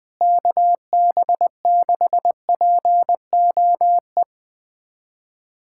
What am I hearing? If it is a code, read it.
Morse code, heard as a keyed tone, about 20 wpm, 700 Hz. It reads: KB6POE
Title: USA Callsign DE Callsign - 6 Characters - 20wpm